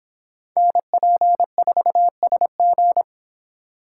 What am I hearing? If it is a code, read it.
Morse code, heard as a keyed tone, about 25 wpm, 700 Hz. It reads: NP4SG